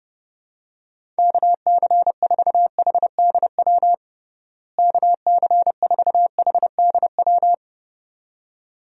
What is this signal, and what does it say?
Morse code, heard as a keyed tone, about 30 wpm, 700 Hz. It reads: KC4HDW KC4HDW
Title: USA Call Signs - 6 Characters - Repeated 2x - 30wpm